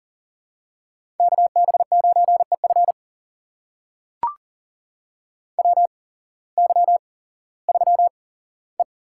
Morse code, heard as a keyed tone, about 40 wpm, spaced as for 19 wpm, 700 Hz. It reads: KB9EF WY3E
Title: USA Call Signs - 8x Character Spacing - 40wpm